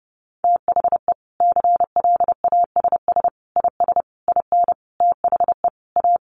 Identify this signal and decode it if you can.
Morse code, heard as a keyed tone, about 30 wpm, 700 Hz. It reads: THE CLASS IS IN THE A
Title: Additional Easy Phrases 30wpm